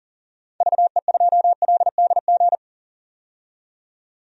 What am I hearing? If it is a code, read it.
Morse code, heard as a keyed tone, about 40 wpm, 700 Hz. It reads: VE2LDG
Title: DXCC-100 Call Signs - 6 Characters - 40wpm